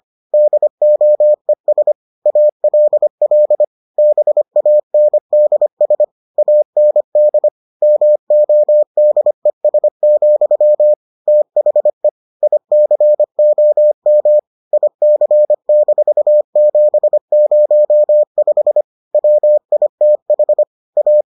Morse code, heard as a keyed tone, about 25 wpm, 600 Hz. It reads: DOES ALL BANDS AND MODES, THE ICOM IC-705 WITH A